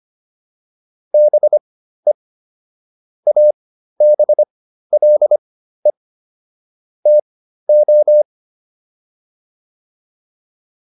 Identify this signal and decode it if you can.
Morse code, heard as a keyed tone, about 25 wpm, spaced as for 13 wpm, 600 Hz. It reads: BE ABLE TO